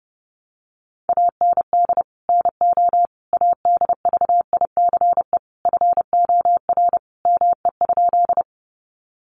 Morse code, heard as a keyed tone, about 30 wpm, 700 Hz. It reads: AND NO ADVICE FOR ME?
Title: Sentences from Top 1100 Words 30wpm